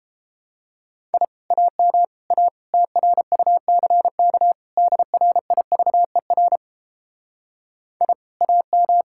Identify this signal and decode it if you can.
Morse code, heard as a keyed tone, about 29 wpm, 700 Hz. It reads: I AM A TRUCK DRIVER I AM